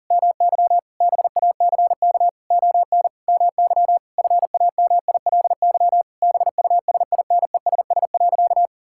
Morse code, heard as a keyed tone, about 40 wpm, 700 Hz. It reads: MY BACK ON MY FAMILY BUSINESS.
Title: Sentences from Top 1000 Words - Review Part 6 - 40wpm